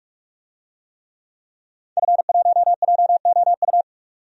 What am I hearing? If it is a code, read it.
Morse code, heard as a keyed tone, about 46 wpm, 700 Hz. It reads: F1JOU
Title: DXCC-100 Call Signs - 5 Characters - 45wpm